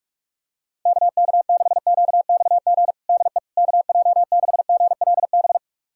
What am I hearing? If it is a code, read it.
Morse code, heard as a keyed tone, about 46 wpm, 700 Hz. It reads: KK6QXG DE KJ6GLB
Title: USA Callsign DE Callsign - 6 Characters - 45wpm